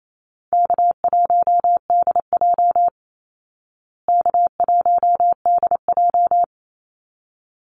Morse code, heard as a keyed tone, about 28 wpm, 700 Hz. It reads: K1DJ K1DJ
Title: Call Signs Repeated 3x - 28wpm